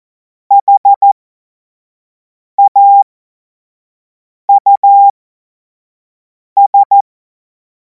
Morse code, heard as a keyed tone, about 14 wpm, spaced as for 5 wpm, 800 Hz. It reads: HAUS
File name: Morsekurs / Witze / Witz 12. Mai 2026 25 F70